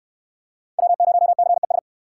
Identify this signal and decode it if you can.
Morse code, heard as a keyed tone, about 63 wpm, 690 Hz. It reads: W0CES